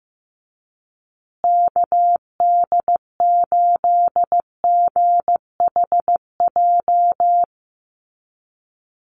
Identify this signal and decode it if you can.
Morse code, heard as a keyed tone, about 15 wpm, 700 Hz. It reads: KD8GHJ
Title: USA Call Signs Repeated 2x 15wpm